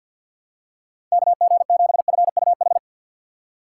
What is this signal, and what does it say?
Morse code, heard as a keyed tone, about 47 wpm, 700 Hz. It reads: KG7FUH
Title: USA Call Signs - 6 Characters - 4x Character Spacing - 50wpm